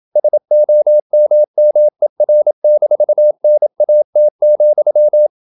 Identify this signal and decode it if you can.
Morse code, heard as a keyed tone, about 27 wpm, 600 Hz. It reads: SOMMER-NAT,
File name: Ferie QRQ 27wpm